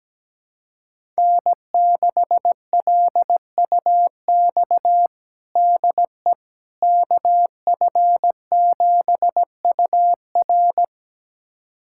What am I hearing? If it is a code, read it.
Morse code, heard as a keyed tone, about 17 wpm, 700 Hz. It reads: N6LUX DE KF7UR
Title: USA Callsign DE Callsign - 5 Characters - 17wpm